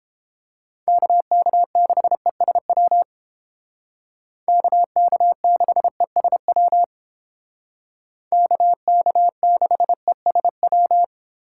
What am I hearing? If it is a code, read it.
Morse code, heard as a keyed tone, about 29 wpm, 700 Hz. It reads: KK6ESW KK6ESW KK6ESW